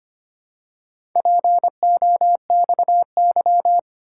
Morse code, heard as a keyed tone, about 25 wpm, 700 Hz. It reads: POXY
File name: practice / 25wpm / Four Letter-Number Combinations - 25wpm